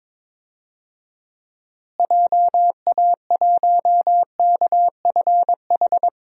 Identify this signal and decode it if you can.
Morse code, heard as a keyed tone, about 22 wpm, 700 Hz. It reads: JA1KFH
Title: DXCC-100 Call Signs - 6 Characters - Repeated 3x - 22wpm